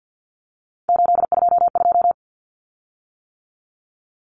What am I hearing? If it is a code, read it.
Morse code, heard as a keyed tone, about 50 wpm, 700 Hz. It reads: 822
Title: Common Cut Numbers - 3-Digit 50wpm